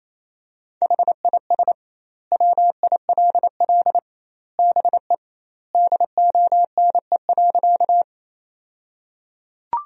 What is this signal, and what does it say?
Morse code, heard as a keyed tone, about 28 wpm, 700 Hz. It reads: HIS WILL BE DONE.
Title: 2x Word-Spacing - Sentences from Top 500 Words - 28wpm